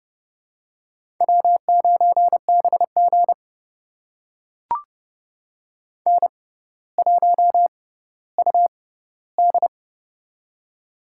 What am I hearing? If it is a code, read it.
Morse code, heard as a keyed tone, about 31 wpm, spaced as for 15 wpm, 700 Hz. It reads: W9BG N1UD
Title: USA Call Signs - 4 Characters - 6x Character Spacing - 30wpm